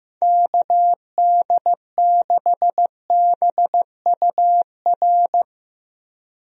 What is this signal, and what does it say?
Morse code, heard as a keyed tone, about 16 wpm, 700 Hz. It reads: KD6BUR
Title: USA Call Signs - 6 Characters - Repeated 2x - 15wpm